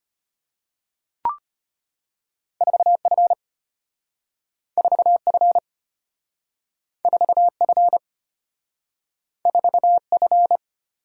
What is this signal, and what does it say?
Morse code, heard as a keyed tone, about 34 wpm, 700 Hz. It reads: 4F 4F 4F 4F